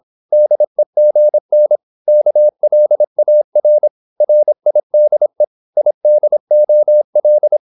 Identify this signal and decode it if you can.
Morse code, heard as a keyed tone, about 26 wpm, 600 Hz. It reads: DEGN KLAR RIDE IDOL
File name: Korte ord DK 26wpm